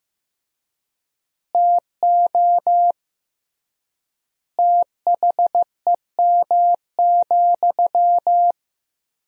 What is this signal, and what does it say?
Morse code, heard as a keyed tone, about 15 wpm, 700 Hz. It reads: TO THEM,
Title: 3x Word-Spacing - Sentences from Top 400 Words - 15wpm